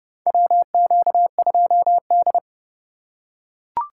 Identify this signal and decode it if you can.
Morse code, heard as a keyed tone, about 30 wpm, 700 Hz. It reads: WQ2D
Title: USA Call Signs - 4 Characters - 8x Character Spacing - 30wpm